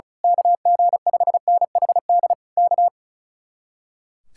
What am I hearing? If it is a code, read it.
Morse code, heard as a keyed tone, about 32 wpm, 700 Hz. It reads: KG5NHD K
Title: Speed-Racing Common QSO Phrases - 35wpm